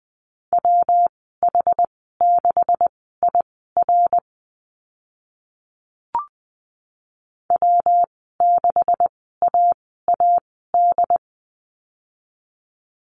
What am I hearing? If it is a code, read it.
Morse code, heard as a keyed tone, about 20 wpm, spaced as for 15 wpm, 700 Hz. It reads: WH6IR W6AAD
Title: 128 Lesson 21g - Number 6 - Call Signs 20wpm